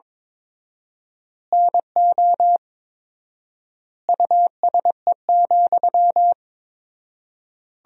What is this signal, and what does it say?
Morse code, heard as a keyed tone, about 22 wpm, 700 Hz. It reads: NO USE,
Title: 4x Word-Spacing - Sentences from Top 500 Words - 22wpm